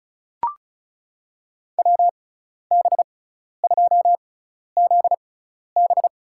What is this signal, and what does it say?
Morse code, heard as a keyed tone, about 36 wpm, spaced as for 12 wpm, 700 Hz. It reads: WB2ZB
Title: USA Call Signs - 6 Characters - 6x Character Spacing - 35wpm